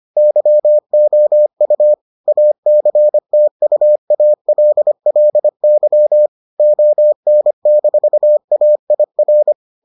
Morse code, heard as a keyed tone, about 25 wpm, 600 Hz. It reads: YOU ACTUALLY ON-AIR